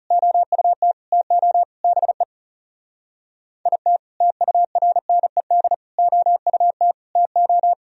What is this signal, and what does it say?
Morse code, heard as a keyed tone, about 35 wpm, 700 Hz. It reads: OUT TO BE IT TURNED OUT TO
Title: Speed-Racing - Sets of 5 Words 30wpm